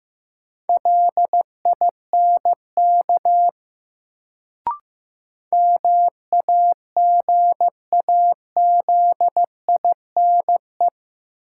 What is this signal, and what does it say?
Morse code, heard as a keyed tone, about 15 wpm, 700 Hz. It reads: LINK MAGAZINE
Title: Sentences from Top 1000 Words - Review Part 7 - New Word Drill - 15wpm